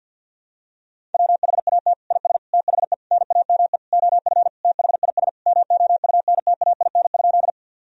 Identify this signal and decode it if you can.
Morse code, heard as a keyed tone, about 51 wpm, 700 Hz. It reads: WHAT IS THE NAME OF THIS MOUNTAIN?
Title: Sentences from Top 1000 Words - Review Part 1 - 50wpm